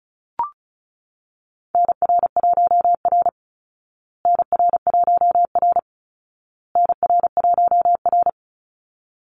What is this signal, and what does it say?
Morse code, heard as a keyed tone, about 34 wpm, 700 Hz. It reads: NR1R NR1R NR1R
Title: Call Signs Repeated 2x - 35wpm